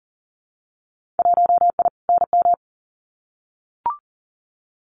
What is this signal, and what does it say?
Morse code, heard as a keyed tone, about 40 wpm, 700 Hz. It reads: 1I NM
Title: Winter Field Day Exchanges Encore 40wpm